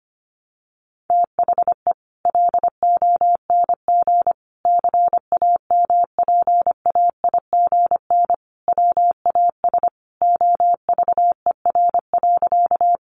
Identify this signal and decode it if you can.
Morse code, heard as a keyed tone, about 25 wpm, 700 Hz. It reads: THE LONG CAMPAIGN WAS OVER.
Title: Sentences from Top 1200 Words 25wpm